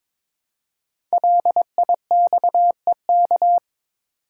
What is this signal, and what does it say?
Morse code, heard as a keyed tone, about 22 wpm, 700 Hz. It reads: LIXEK
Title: Code Groups - 5 Letter:Numbers - 22wpm